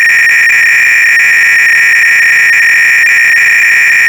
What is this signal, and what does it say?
300-baud modem data.